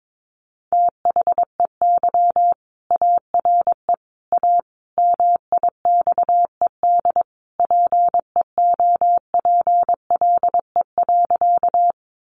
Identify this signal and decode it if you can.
Morse code, heard as a keyed tone, about 22 wpm, 700 Hz. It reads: THEY ARE A MIXED PEOPLE.
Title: Sentences from Top 1500 Words 22wpm